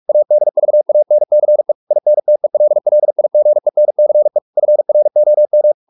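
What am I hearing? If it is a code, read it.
Morse code, heard as a keyed tone, about 44 wpm, 600 Hz. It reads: ADVANCE INTELLIGENCE FROM
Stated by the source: Morse practice at 45 wpm, the novel Never by Ken Follett in English, from CW med Gnister.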